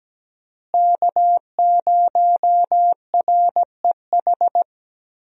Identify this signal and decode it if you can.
Morse code, heard as a keyed tone, about 17 wpm, 700 Hz. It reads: K0REH
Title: USA Call Signs - 5 Characters - Repeated 2x - 17wpm